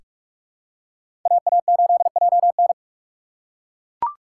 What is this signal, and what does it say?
Morse code, heard as a keyed tone, about 41 wpm, 700 Hz. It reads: AA8JN
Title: USA Call Signs - 5 Characters - 4x Character Spacing - 45wpm